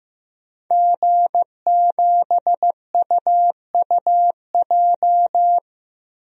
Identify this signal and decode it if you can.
Morse code, heard as a keyed tone, about 15 wpm, 700 Hz. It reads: G7UUJ
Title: DXCC-100 Call Signs - 5 Characters - Repeated 2x - 15wpm